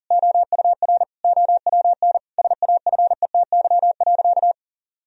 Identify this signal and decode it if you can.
Morse code, heard as a keyed tone, about 37 wpm, 700 Hz. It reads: OUR OWN SAFETY.
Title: Sentences from Top 1000 Words - Review Part 4 - 40wpm